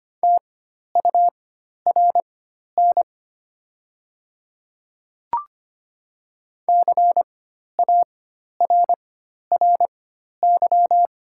Morse code, heard as a keyed tone, about 25 wpm, spaced as for 12 wpm, 700 Hz. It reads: TURN CARRY
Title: 4x Character Spacing - Top 300 Words - 25wpm